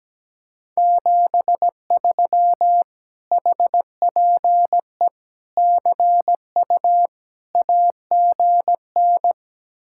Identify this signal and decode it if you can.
Morse code, heard as a keyed tone, about 17 wpm, 700 Hz. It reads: 73 HPE CU AGN
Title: Speed-Racing Common QSO Phrases - 17wpm